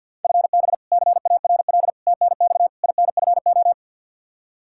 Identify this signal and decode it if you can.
Morse code, heard as a keyed tone, about 52 wpm, 700 Hz. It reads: FB CARL TNX INFO
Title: Practice QSO - Exchange 3 - More Info - 50wpm